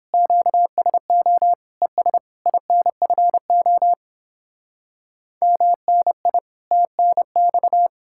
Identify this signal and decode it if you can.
Morse code, heard as a keyed tone, about 29 wpm, 700 Hz. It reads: QSO ES INFO MNI TNX